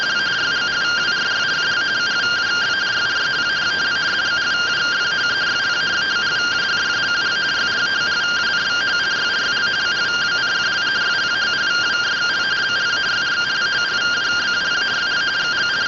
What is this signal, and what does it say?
Two-tone teleprinter signal, about 170 Hz shift.
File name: rtty rx